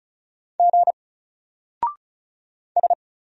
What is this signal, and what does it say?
Morse code, heard as a keyed tone, about 33 wpm, 700 Hz. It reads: G S
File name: Single-Letter-Number-35wpm